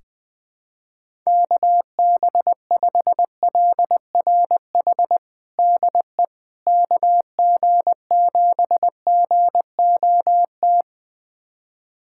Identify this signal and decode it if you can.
Morse code, heard as a keyed tone, about 21 wpm, 700 Hz. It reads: KB5LRH DE KG7GOT
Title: USA Callsign DE Callsign 20wpm